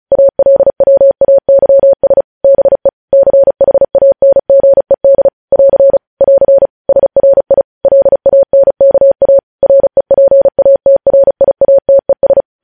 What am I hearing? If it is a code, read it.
Morse code, heard as a keyed tone, about 36 wpm, 550 Hz. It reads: ALWAYS BE CHANGED <AR> <AR> SRI LANKA REPATRIATES